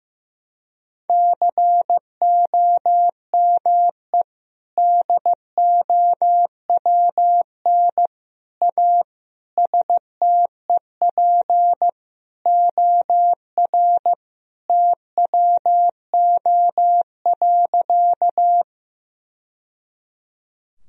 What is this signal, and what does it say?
Morse code, heard as a keyed tone, about 15 wpm, 700 Hz. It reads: COME DOWN A STEP OR TWO.
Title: Sentences from Top 1000 Words - Review Part 2 - 15wpm